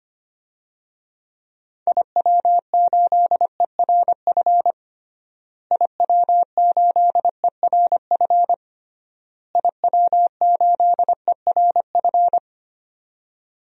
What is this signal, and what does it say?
Morse code, heard as a keyed tone, about 25 wpm, 700 Hz. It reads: IW8ERF IW8ERF IW8ERF
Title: DXCC-100 Call Signs - 6 Characters - Repeated 3x - 25wpm